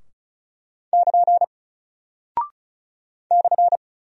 Morse code, heard as a keyed tone, about 36 wpm, 700 Hz. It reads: <KN> /
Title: Single Character (Letter:Number:Punctuation:Prosign) Alt 35wpm